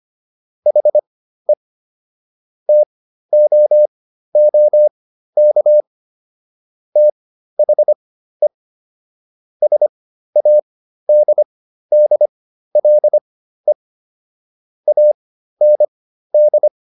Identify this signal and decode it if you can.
Morse code, heard as a keyed tone, about 25 wpm, spaced as for 13 wpm, 600 Hz. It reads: HE TOOK THE SADDLE AND